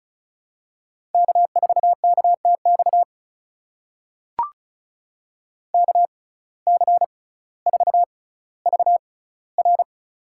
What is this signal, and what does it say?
Morse code, heard as a keyed tone, about 36 wpm, spaced as for 18 wpm, 700 Hz. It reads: K4KTX KC4VR
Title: USA Call Signs - 5 Characters - 6x Character Spacing - 35wpm